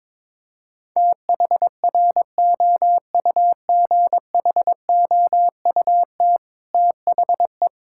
Morse code, heard as a keyed tone, about 22 wpm, 700 Hz. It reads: THROUGHOUT THE
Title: Sentences from Top 1000 Words - Review Part 5 - 22wpm